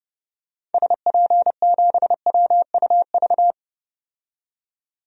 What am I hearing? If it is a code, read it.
Morse code, heard as a keyed tone, about 30 wpm, 700 Hz. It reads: SP7WUV
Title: DXCC-100 Call Signs - 6 Characters - 30wpm